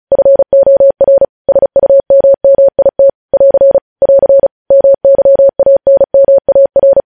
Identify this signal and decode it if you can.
Morse code, heard as a keyed tone, about 36 wpm, 550 Hz. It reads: FOR SUMMIT <AR> <AR> MYANMAR
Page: News headlines in Morse code at 35 wpm